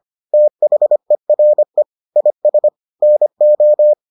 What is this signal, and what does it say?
Morse code, heard as a keyed tone, about 25 wpm, 600 Hz. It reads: THERE IS NO